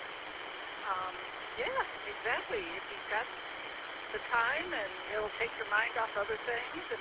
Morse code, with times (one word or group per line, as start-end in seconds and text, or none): none